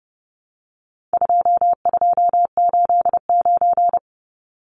Morse code, heard as a keyed tone, about 30 wpm, 700 Hz. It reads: 2289
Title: Numbers - 4-Digit 30wpm